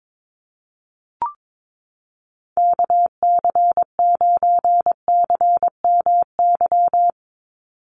Morse code, heard as a keyed tone, about 22 wpm, 700 Hz. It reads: KC9CMY
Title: USA Call Signs 22wpm